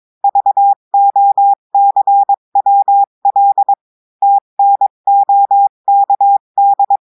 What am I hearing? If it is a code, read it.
Morse code, heard as a keyed tone, about 22 wpm, 800 Hz. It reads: VOCWL TNOKD
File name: cours-du-mercredi / audios / cours-853-du-2-mai-2024-etxte chiffre et texte clair-a-20-wpm